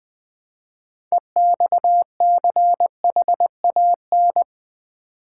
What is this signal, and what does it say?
Morse code, heard as a keyed tone, about 20 wpm, 700 Hz. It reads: EXCHAN
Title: Build A Word - 3 Letters At A Time - 8-Letter Words - 20wpm